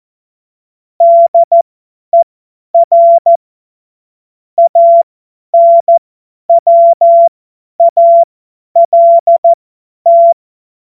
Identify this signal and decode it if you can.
Morse code, heard as a keyed tone, about 14 wpm, spaced as for 10 wpm, 680 Hz. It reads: DER ANWALT
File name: Morsekurs / Witze / Witz 17. Mai 2022 50 F70